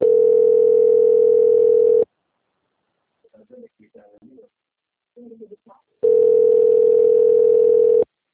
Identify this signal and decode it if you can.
Ringback tone (North America), 440 + 480 Hz, with speech.
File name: out- - - - - .8904